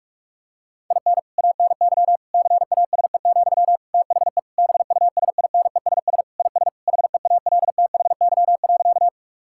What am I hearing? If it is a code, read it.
Morse code, heard as a keyed tone, about 46 wpm, 700 Hz. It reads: IN ANY CASE, THE BUSINESS IS HEALTHY.
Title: Sentences from Top 1000 Words - Review Part 5 - 45wpm